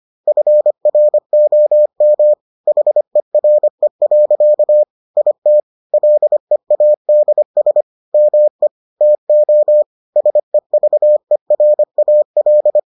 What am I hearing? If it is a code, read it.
Morse code, heard as a keyed tone, about 25 wpm, 600 Hz. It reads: FROM HERE. IT LEADS ME TO SEVERAL